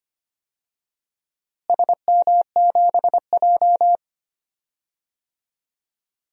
Morse code, heard as a keyed tone, about 25 wpm, 700 Hz. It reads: SM7J
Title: DXCC-100 Call Signs - 4 Characters - Repeated 3x - 25wpm